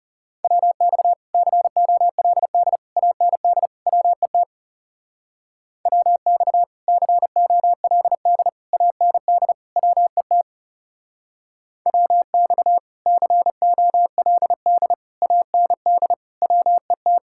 Morse code, heard as a keyed tone, about 35 wpm, 700 Hz. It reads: WX COLD AND WET WX COLD AND WET WX COLD AND WET
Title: Speed-Racing Common QSO Phrases Encore - 30wpm